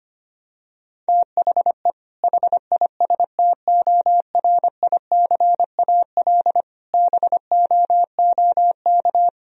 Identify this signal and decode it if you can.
Morse code, heard as a keyed tone, about 25 wpm, 700 Hz. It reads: THE HISTORICAL BOOK
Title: Sentences from Top 1000 Words - Review Part 4 - 25wpm